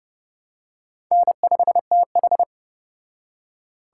Morse code, heard as a keyed tone, about 31 wpm, 700 Hz. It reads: N5TH